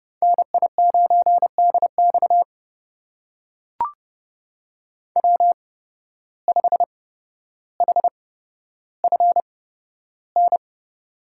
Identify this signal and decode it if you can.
Morse code, heard as a keyed tone, about 31 wpm, spaced as for 13 wpm, 700 Hz. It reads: NI9DX W5HFN
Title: USA Call Signs - 5 Characters - 8x Character Spacing - 30wpm